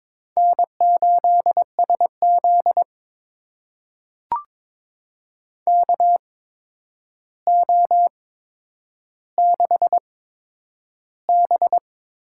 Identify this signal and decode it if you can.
Morse code, heard as a keyed tone, about 22 wpm, spaced as for 9 wpm, 700 Hz. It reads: N8SZ KO6B
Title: USA Call Signs - 4 Characters - 8x Character Spacing - 22wpm